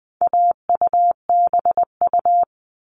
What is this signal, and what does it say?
Morse code, heard as a keyed tone, about 20 wpm, 700 Hz. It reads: AUBU
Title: All Cut Numbers - 4-Digit 20wpm